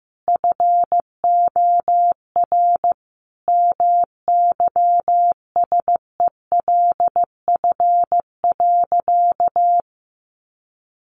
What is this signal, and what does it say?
Morse code, heard as a keyed tone, about 15 wpm, 700 Hz. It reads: FOR MYSELF.
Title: Sentences from Top 1300 Words 15wpm